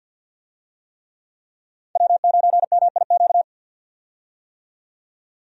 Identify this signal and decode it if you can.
Morse code, heard as a keyed tone, about 46 wpm, 700 Hz. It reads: W9MIQ
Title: USA Call Signs - 5 Characters - 50wpm